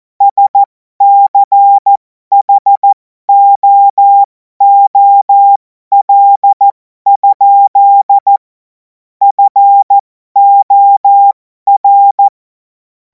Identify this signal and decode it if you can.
Morse code, heard as a keyed tone, about 14 wpm, spaced as for 12 wpm, 800 Hz. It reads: SCHOOL? FOR
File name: Morsekurs / Witze